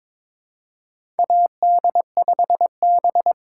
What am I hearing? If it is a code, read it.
Morse code, heard as a keyed tone, about 22 wpm, 700 Hz. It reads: AD5B
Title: USA Call Signs - 4 Characters - 2x Character Spacing - 22wpm